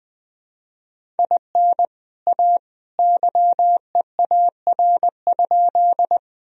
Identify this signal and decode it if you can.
Morse code, heard as a keyed tone, about 21 wpm, 700 Hz. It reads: IN A YEAR?
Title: Sentences from Top 1000 Words - Review Part 1 - 20wpm